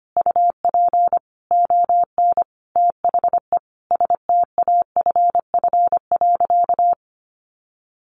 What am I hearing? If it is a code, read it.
Morse code, heard as a keyed tone, about 25 wpm, 700 Hz. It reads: UP ON THE STAFF.
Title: Sentences from Top 1200 Words 25wpm